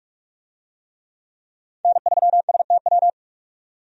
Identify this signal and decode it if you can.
Morse code, heard as a keyed tone, about 46 wpm, 700 Hz. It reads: N3STW
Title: USA Call Signs - 5 Characters - 45wpm